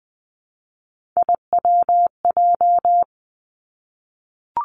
Morse code, heard as a keyed tone, about 20 wpm, 700 Hz. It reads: IWJ
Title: Three Challenging Letter-Number Combinations 20wpm